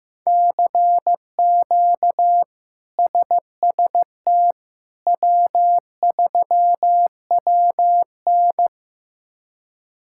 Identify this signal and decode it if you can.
Morse code, heard as a keyed tone, about 15 wpm, 700 Hz. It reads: CQ SST W3WN
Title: K1USN SST QSO Practice - 15wpm